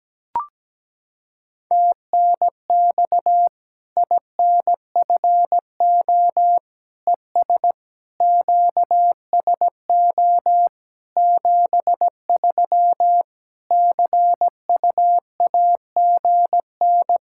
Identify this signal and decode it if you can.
Morse code, heard as a keyed tone, about 17 wpm, 700 Hz. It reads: TNX INFO ES QSO 73 CUAGN
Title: K1USN SST QSO Practice 17wpm